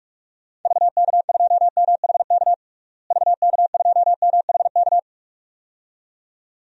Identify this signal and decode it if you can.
Morse code, heard as a keyed tone, about 42 wpm, 700 Hz. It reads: VK2MHK VK2MHK
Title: DXCC-100 Call Signs - 6 Characters - Repeated 2x - 45wpm